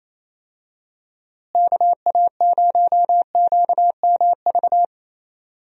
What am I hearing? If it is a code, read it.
Morse code, heard as a keyed tone, about 28 wpm, 700 Hz. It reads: KA0QMV
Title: USA Call Signs - 6 Characters - Repeated 3x - 28wpm